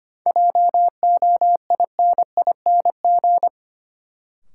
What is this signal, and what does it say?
Morse code, heard as a keyed tone, about 25 wpm, 700 Hz. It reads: JOINING
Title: Sets of 5 Words - New Word Drill - 25wpm